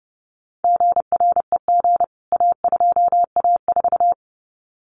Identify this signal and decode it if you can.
Morse code, heard as a keyed tone, about 30 wpm, 700 Hz. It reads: GREG A2A4
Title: CWT Exchange 30wpm